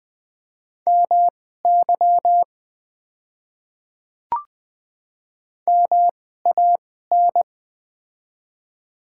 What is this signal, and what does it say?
Morse code, heard as a keyed tone, about 20 wpm, spaced as for 14 wpm, 700 Hz. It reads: MY MAN
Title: 2x Character Spacing - Top 200 Words - 20wpm